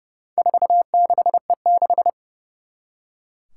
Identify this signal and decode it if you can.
Morse code, heard as a keyed tone, about 30 wpm, 700 Hz. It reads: 46E6
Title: Four Challenging Letter-Number Combinations 30wpm